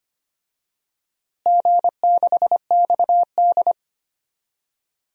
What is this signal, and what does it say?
Morse code, heard as a keyed tone, about 25 wpm, 700 Hz. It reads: G6XD
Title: DXCC-100 Call Signs - 4 Characters - Repeated 3x - 25wpm